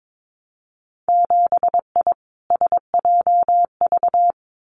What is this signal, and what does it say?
Morse code, heard as a keyed tone, about 22 wpm, 700 Hz. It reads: 7I SJV
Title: Winter Field Day Exchanges Encore 22wpm